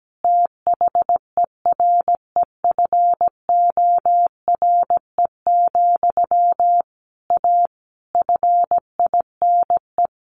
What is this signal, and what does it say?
Morse code, heard as a keyed tone, about 17 wpm, 700 Hz. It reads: THEREFORE, A FINE
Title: Sentences from Top 1400 Words 17wpm